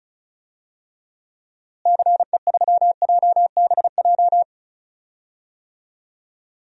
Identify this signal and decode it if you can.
Morse code, heard as a keyed tone, about 34 wpm, 700 Hz. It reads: CE3JBJ
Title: DXCC-100 Call Signs - 6 Characters - Repeated 2x - 35wpm